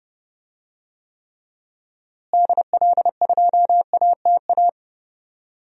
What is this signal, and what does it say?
Morse code, heard as a keyed tone, about 30 wpm, 700 Hz. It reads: DL2ATA
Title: DXCC-100 Call Signs - 6 Characters - Repeated 2x - 30wpm